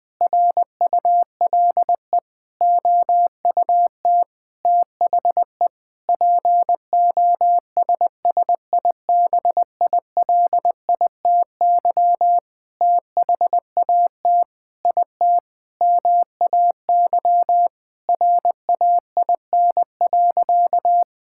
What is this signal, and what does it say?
Morse code, heard as a keyed tone, about 21 wpm, 700 Hz. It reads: RULE OUT THE POSSIBILITY THAT IT MAY RAIN.
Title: Sentences with 8-Word Phrases - Spoken Beginning - 20wpm